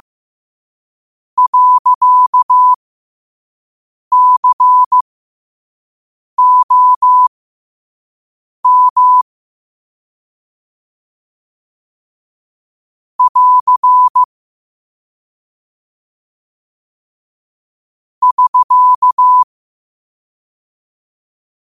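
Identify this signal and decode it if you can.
Morse code, heard as a keyed tone, about 15 wpm, spaced as for 5 wpm, 1000 Hz. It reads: .COM <AR> <SK>
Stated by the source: Morse code sent at 5 wpm.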